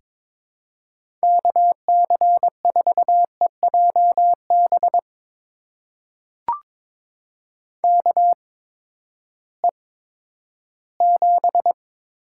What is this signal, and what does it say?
Morse code, heard as a keyed tone, about 21 wpm, spaced as for 12 wpm, 700 Hz. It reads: KC4EJB KE7
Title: USA Call Signs - 8x Character Spacing - 22wpm